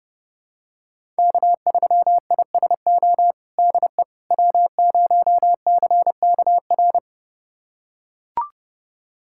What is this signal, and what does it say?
Morse code, heard as a keyed tone, about 32 wpm, 700 Hz. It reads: K3ISO DE W0CKR
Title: USA Callsign DE Callsign - 5 Characters - 30wpm